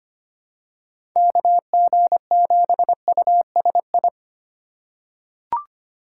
Morse code, heard as a keyed tone, about 25 wpm, 700 Hz. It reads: KG7USI
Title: USA Call Signs - 6x Character Spacing - 25wpm